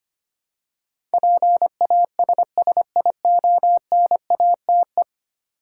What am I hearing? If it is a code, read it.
Morse code, heard as a keyed tone, about 25 wpm, 700 Hz. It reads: PASSIONATE
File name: Ten Letter Words - 25wpm